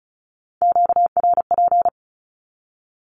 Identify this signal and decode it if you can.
Morse code, heard as a keyed tone, about 36 wpm, 700 Hz. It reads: QRP
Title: Q-Codes with Spoken Meaning - 35wpm